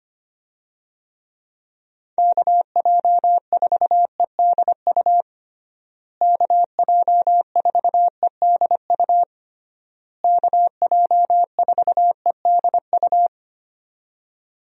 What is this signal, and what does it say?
Morse code, heard as a keyed tone, about 25 wpm, 700 Hz. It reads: KJ4EDU KJ4EDU KJ4EDU
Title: USA Call Signs - 6 Characters - Repeated 3x - 25wpm